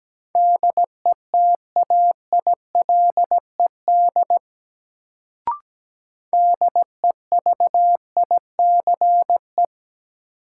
Morse code, heard as a keyed tone, about 17 wpm, 700 Hz. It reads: DETAILED DEVICE
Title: Sentences from Top 1500 Words - New Word Drill Alt - 17wpm